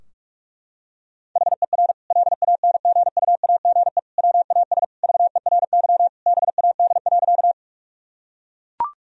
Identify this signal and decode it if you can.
Morse code, heard as a keyed tone, about 41 wpm, 700 Hz. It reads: HER LANGUAGE WAS VERY BAD.
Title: Sentences from Top 1000 Words - Review Part 3 - 45wpm